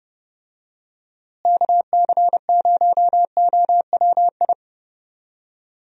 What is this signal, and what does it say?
Morse code, heard as a keyed tone, about 30 wpm, 700 Hz. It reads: KC0OWI
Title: USA Call Signs - 6 Characters - Repeated 3x - 30wpm